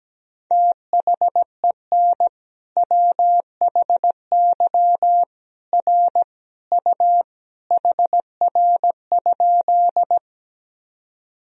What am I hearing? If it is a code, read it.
Morse code, heard as a keyed tone, about 17 wpm, 700 Hz. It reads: THEN WHY R U HR?
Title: Sentences from Top 200 Words and Abbreviations 17wpm